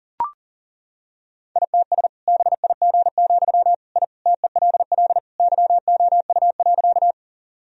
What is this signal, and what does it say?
Morse code, heard as a keyed tone, about 40 wpm, 700 Hz. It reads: ITS BIG, I TELL YOU.
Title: Sentences from Top 1000 Words - Review Part 1 - 40wpm